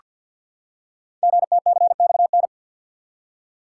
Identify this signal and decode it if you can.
Morse code, heard as a keyed tone, about 48 wpm, 690 Hz. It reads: GTCXN
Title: Code Groups - 5 Letters - 50wpm